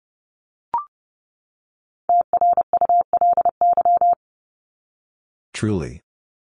Morse code, heard as a keyed tone, about 30 wpm, 700 Hz. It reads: TRULY
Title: Top 1501-1750 Words 30wpm